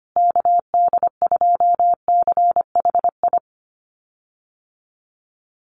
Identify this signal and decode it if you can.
Morse code, heard as a keyed tone, about 25 wpm, 700 Hz. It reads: KD2CHI
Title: USA Call Signs Repeated 2x 25wpm